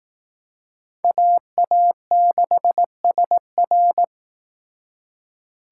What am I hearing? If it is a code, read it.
Morse code, heard as a keyed tone, about 19 wpm, 700 Hz. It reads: AA6SR
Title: Speed Racing - USA 2x2 Call Signs 15wpm